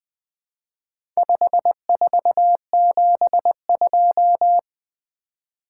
Morse code, heard as a keyed tone, about 20 wpm, 700 Hz. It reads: 5472
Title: Four Letter-Number Combinations - 20wpm